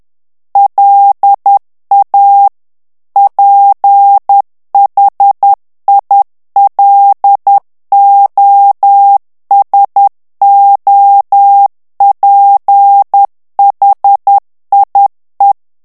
Morse code, heard as a keyed tone, about 10 wpm, 800 Hz. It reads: LA PHILOSOPHIE
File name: CW / FICHIERS-MP3 / FRANCAIS TEXTE CLAIR 2 VITESSE 10 MOTS MINUTES